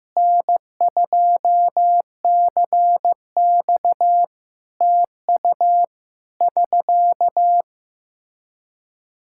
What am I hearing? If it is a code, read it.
Morse code, heard as a keyed tone, about 15 wpm, 700 Hz. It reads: N2CX TU <SK>
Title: Practice QSO - Exchange 4 - Ending - 15wpm